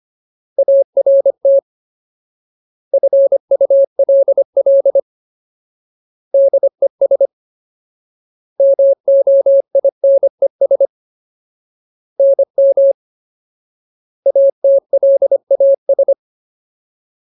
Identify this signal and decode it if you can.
Morse code, heard as a keyed tone, about 25 wpm, 550 Hz. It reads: ART FULL DES MOINES NM ATLAS
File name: Ultimate Word List / 25WPM Ultimate